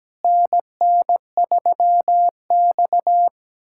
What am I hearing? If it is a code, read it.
Morse code, heard as a keyed tone, about 17 wpm, 700 Hz. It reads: NN3X
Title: USA Callsign DE Callsign - 4 Characters - 17wpm